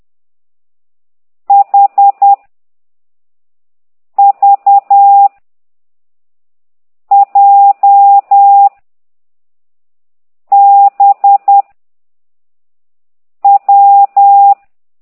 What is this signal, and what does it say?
Morse code, heard as a keyed tone, about 10 wpm, 800 Hz. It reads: H V J B W